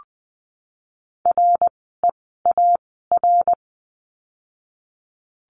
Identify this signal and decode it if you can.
Morse code, heard as a keyed tone, about 20 wpm, spaced as for 14 wpm, 700 Hz. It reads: REAR
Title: 061 Lesson 11f - Letter h - Words 20wpm